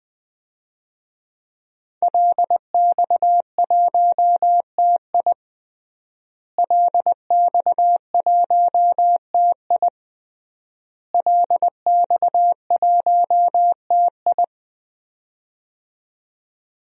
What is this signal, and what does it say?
Morse code, heard as a keyed tone, about 20 wpm, 700 Hz. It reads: LX1TI LX1TI LX1TI
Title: DXCC-100 Call Signs - 5 Characters - Repeated 3x - 20wpm